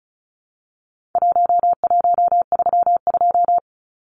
Morse code, heard as a keyed tone, about 34 wpm, 700 Hz. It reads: 1132
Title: Numbers - Contest Style - 4-Digit 35wpm